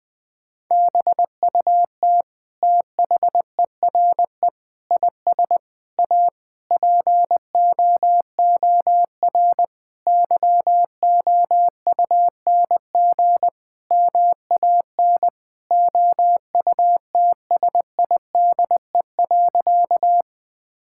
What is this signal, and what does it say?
Morse code, heard as a keyed tone, about 21 wpm, 700 Hz. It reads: BUT THERE IS A POOR YOUNG MAN OUTSIDE.
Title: Sentences from Top 1000 Words - Review Part 6 - 20wpm